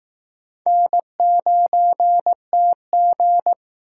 Morse code, heard as a keyed tone, about 18 wpm, 700 Hz. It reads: N9TG
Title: Speed Racing - USA 1x2 Call Signs 15wpm